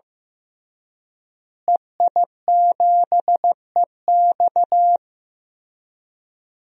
Morse code, heard as a keyed tone, about 15 wpm, 700 Hz. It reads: EI7EX
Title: DXCC-100 Call Signs - 5 Characters - Repeated 3x - 15wpm